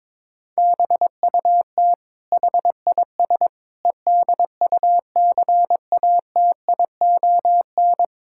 Morse code, heard as a keyed tone, about 22 wpm, 700 Hz. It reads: BUT HIS EDUCATION
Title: Sentences from Top 1000 Words - Review Part 6 - 22wpm